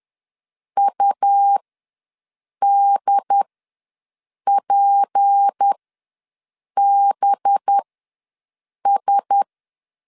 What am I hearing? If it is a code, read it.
Morse code, heard as a keyed tone, about 10 wpm, spaced as for 6 wpm, 800 Hz. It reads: UDPBS
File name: LEZIONI / 6 PKX0 / 6 P ws 10wpm 3751